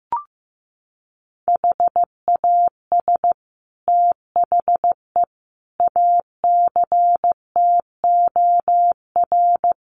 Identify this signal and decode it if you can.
Morse code, heard as a keyed tone, about 15 wpm, 700 Hz. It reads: HAS THE ACTOR
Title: Sentences from Top 1500 Words 15wpm